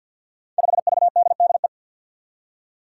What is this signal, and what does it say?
Morse code, heard as a keyed tone, about 45 wpm, 700 Hz. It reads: 5VDDE
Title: Five Challenging Letter-Number Combinations 50wpm